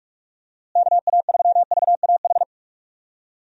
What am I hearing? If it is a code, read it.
Morse code, heard as a keyed tone, about 46 wpm, 700 Hz. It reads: KA3VAH
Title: USA Call Signs - 6 Characters - Repeated 2x - 45wpm